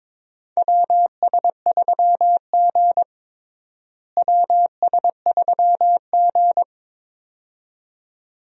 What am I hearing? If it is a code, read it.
Morse code, heard as a keyed tone, about 22 wpm, 700 Hz. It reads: WS3G WS3G
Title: USA Call Signs - 4 Characters - Repeated 2x - 22wpm